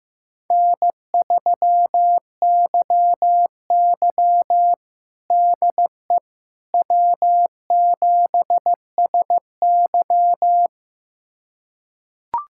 Morse code, heard as a keyed tone, about 15 wpm, 700 Hz. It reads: N3YY DE W7SY
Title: USA Callsign DE Callsign - 4 Characters - 15wpm